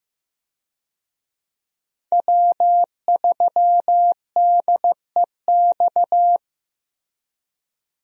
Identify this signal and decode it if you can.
Morse code, heard as a keyed tone, about 15 wpm, 700 Hz. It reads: W3DEX
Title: USA Call Signs - 5 Characters - Repeated 2x - 15wpm